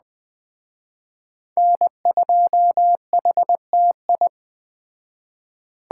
Morse code, heard as a keyed tone, about 20 wpm, 700 Hz. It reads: N2HTI
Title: Speed Racing - USA Call Signs 17wpm